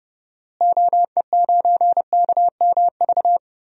Morse code, heard as a keyed tone, about 32 wpm, 700 Hz. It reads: OE9KMV
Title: DXCC-100 Call Signs - 6 Characters - Repeated 3x - 30wpm